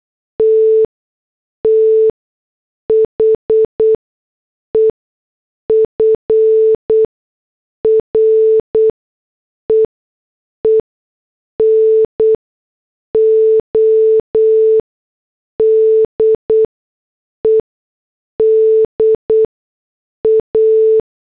Morse code, heard as a keyed tone, about 8 wpm, spaced as for 6 wpm, 440 Hz. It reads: T THE FREENODE DA